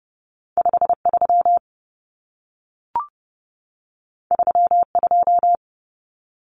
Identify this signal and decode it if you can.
Morse code, heard as a keyed tone, about 30 wpm, 700 Hz. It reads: 53 32
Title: Common Cut Numbers - 2-Digit 30wpm